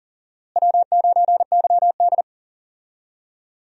Morse code, heard as a keyed tone, about 40 wpm, 700 Hz. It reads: W9YD
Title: USA Call Signs - 4 Characters - Repeated 3x - 40wpm